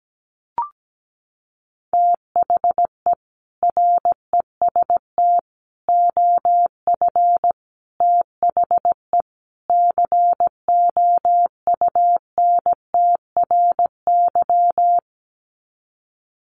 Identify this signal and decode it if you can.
Morse code, heard as a keyed tone, about 17 wpm, 700 Hz. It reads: THE REST OF THE COUNTRY
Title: Sets of 5 Words 17wpm